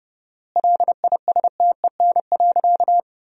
Morse code, heard as a keyed tone, about 30 wpm, 700 Hz. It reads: LISTEN.
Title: 4x Word-Spacing - Sentences from Top 500 Words - 30wpm